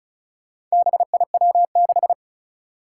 Morse code, heard as a keyed tone, about 36 wpm, 700 Hz. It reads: BIW6
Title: Four Challenging Letter-Number Combinations 35wpm